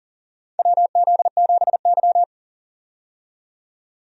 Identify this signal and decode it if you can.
Morse code, heard as a keyed tone, about 40 wpm, 700 Hz. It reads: WZ7Y
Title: USA Call Signs - 4 Characters - 40wpm